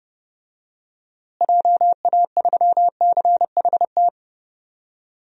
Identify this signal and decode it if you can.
Morse code, heard as a keyed tone, about 31 wpm, 700 Hz. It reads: JA3CHT
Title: DXCC-100 Call Signs - 6 Characters - Repeated 3x - 30wpm